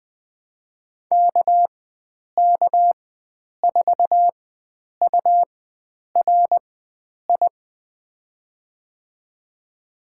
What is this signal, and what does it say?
Morse code, heard as a keyed tone, about 21 wpm, spaced as for 9 wpm, 700 Hz. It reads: KK4URI
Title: USA Call Signs - 6 Characters - 4x Character Spacing - 20wpm